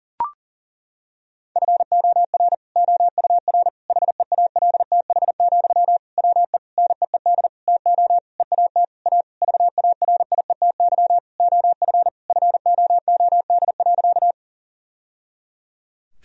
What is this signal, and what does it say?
Morse code, heard as a keyed tone, about 40 wpm, 700 Hz. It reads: FOR OUR HEALTH, WE NEED TO EAT A VARIETY OF FOOD.
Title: Sentences from Top 1000 Words - Review Part 4 - 40wpm